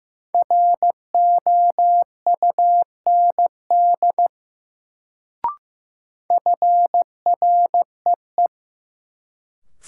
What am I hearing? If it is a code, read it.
Morse code, heard as a keyed tone, about 15 wpm, 700 Hz. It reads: ROUND FREE
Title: Sets of 5 Words Encore - New Word Drill - 15wpm